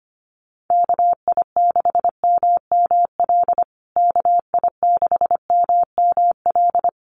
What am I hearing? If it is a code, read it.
Morse code, heard as a keyed tone, about 25 wpm, 700 Hz. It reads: KI6MML KI6MML
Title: USA Call Signs Repeated 3x 25wpm